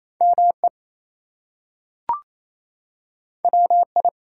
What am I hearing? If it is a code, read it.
Morse code, heard as a keyed tone, about 28 wpm, 700 Hz. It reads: ME WI
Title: US States and Canadian Provinces 28wpm